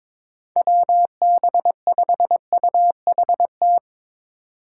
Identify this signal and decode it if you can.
Morse code, heard as a keyed tone, about 22 wpm, 700 Hz. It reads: WB5UHT
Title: USA Call Signs - 6 Characters - 3x Character Spacing - 22wpm